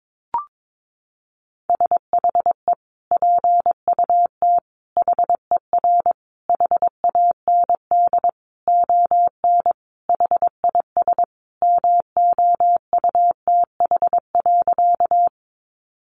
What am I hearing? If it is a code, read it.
Morse code, heard as a keyed tone, about 22 wpm, 700 Hz. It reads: SHE PUT HER HAND ON HIS MOUTH.
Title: Sentences from Top 1500 Words 22wpm